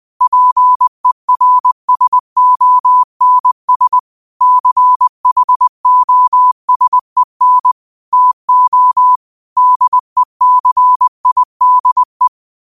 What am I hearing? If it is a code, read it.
Morse code, heard as a keyed tone, about 20 wpm, 1000 Hz. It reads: PERSONS CHOSEN TO DECIDE